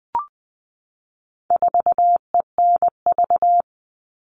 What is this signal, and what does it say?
Morse code, heard as a keyed tone, about 20 wpm, 700 Hz. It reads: 4ENV
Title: All Cut Numbers - 4-Digit 20wpm